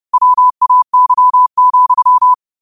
Morse code, heard as a keyed tone, about 30 wpm, 1000 Hz. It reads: WAY,
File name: qotdfiles / 30wpm